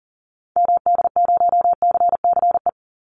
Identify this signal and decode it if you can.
Morse code, heard as a keyed tone, about 37 wpm, 700 Hz. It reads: MD0CCE
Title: Call Signs - 40wpm